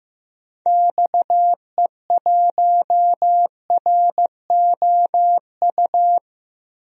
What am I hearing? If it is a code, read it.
Morse code, heard as a keyed tone, about 15 wpm, 700 Hz. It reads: XE1ROU
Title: DXCC-100 Call Signs - 6 Characters - Repeated 3x - 15wpm